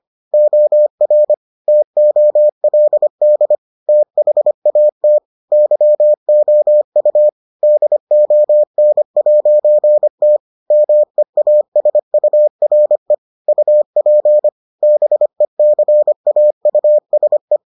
Morse code, heard as a keyed tone, about 25 wpm, 600 Hz. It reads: OR TOLD THAT YOU DON'T MEASURE UP BECAUSE